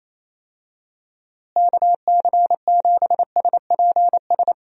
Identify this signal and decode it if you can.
Morse code, heard as a keyed tone, about 28 wpm, 700 Hz. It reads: KC7SPS
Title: USA Call Signs - 6 Characters - Repeated 3x - 28wpm